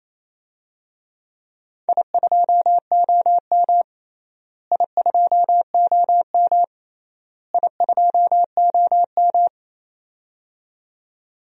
Morse code, heard as a keyed tone, about 28 wpm, 700 Hz. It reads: I2OM I2OM I2OM
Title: DXCC-100 Call Signs - 4 Characters - Repeated 3x - 28wpm